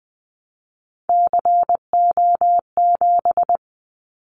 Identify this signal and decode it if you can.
Morse code, heard as a keyed tone, about 20 wpm, 700 Hz. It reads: CO7
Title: Callsign Prefixes for Top 100 DXCC - 20wpm